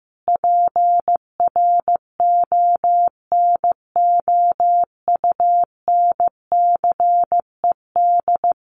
Morse code, heard as a keyed tone, about 15 wpm, 700 Hz. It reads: PRONOUNCED
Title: A Sound Of Thunder 15wpm